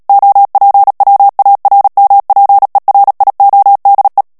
Morse code, heard as a keyed tone, about 37 wpm, 800 Hz. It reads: OPWARMPERIODE
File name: morse mp3 / HOLLANDAIS texte clair 2 vitesse 35 mots minute